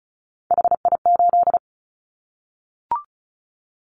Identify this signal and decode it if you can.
Morse code, heard as a keyed tone, about 36 wpm, 700 Hz. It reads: HI8
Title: Callsign Prefixes for Top 100 DXCC - 35wpm